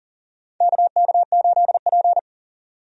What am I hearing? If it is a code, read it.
Morse code, heard as a keyed tone, about 40 wpm, 700 Hz. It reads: KK8P
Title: USA Call Signs - 4 Characters - 40wpm